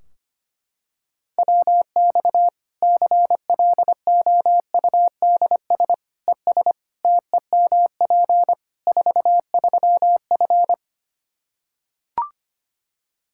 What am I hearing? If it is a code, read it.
Morse code, heard as a keyed tone, about 25 wpm, 700 Hz. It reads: WX CLOUDS ES TEMP 43F
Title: K1USN SST QSO Practice 25wpm